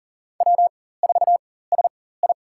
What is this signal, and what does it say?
Morse code, heard as a keyed tone, about 40 wpm, spaced as for 19 wpm, 700 Hz. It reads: W4SI
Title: USA Call Signs - 4 Characters - 4x Character Spacing - 40wpm